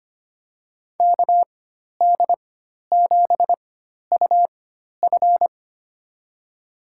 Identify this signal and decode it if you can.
Morse code, heard as a keyed tone, about 25 wpm, spaced as for 12 wpm, 700 Hz. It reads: KD7UF
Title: USA Call Signs - 4x Character Spacing - 25wpm